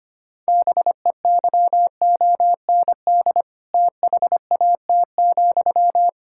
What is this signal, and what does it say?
Morse code, heard as a keyed tone, about 25 wpm, 700 Hz. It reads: BEYOND THAT,
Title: Sentences from Top 1000 Words - Review Part 6 - 25wpm